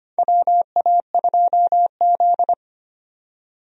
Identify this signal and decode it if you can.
Morse code, heard as a keyed tone, about 25 wpm, 700 Hz. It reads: WA2Z
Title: USA Call Signs - 4 Characters - 25wpm